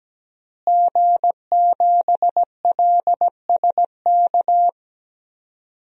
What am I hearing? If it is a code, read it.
Morse code, heard as a keyed tone, about 17 wpm, 700 Hz. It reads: G7LSK
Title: DXCC-100 Call Signs - 5 Characters - Repeated 2x - 17wpm